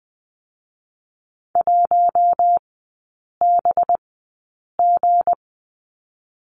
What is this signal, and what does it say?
Morse code, heard as a keyed tone, about 20 wpm, spaced as for 8 wpm, 700 Hz. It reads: 1BG
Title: 218 Lesson 35e - Number 0 - Three Characters 20wpm